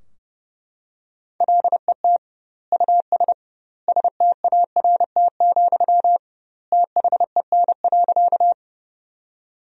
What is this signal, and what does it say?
Morse code, heard as a keyed tone, about 32 wpm, 700 Hz. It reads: LET US START, THEN.
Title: 2x Word-Spacing - Sentences from Top 300 Words - 30wpm